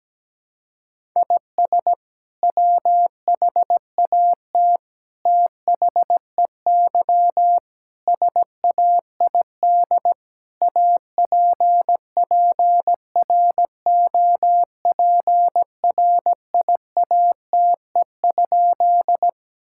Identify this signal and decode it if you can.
Morse code, heard as a keyed tone, about 17 wpm, 700 Hz. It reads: IS WHAT THEY SAID APPROPRIATE?
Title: Sentences from Top 1000 Words - Review Part 5 - 17wpm